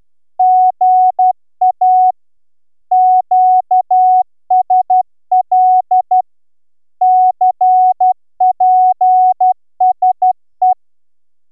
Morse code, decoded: GA QSL CPSE